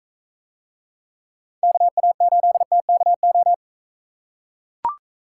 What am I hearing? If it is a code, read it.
Morse code, heard as a keyed tone, about 43 wpm, 700 Hz. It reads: KA8TKO